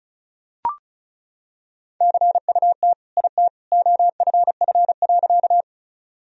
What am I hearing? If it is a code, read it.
Morse code, heard as a keyed tone, about 36 wpm, 700 Hz. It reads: CUT IT OFF.
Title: Sentences from Top 1000 Words - Review Part 1 - 35wpm